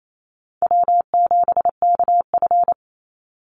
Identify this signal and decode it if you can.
Morse code, heard as a keyed tone, about 28 wpm, 700 Hz. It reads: W7KF
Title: Easy Call Signs - 28wpm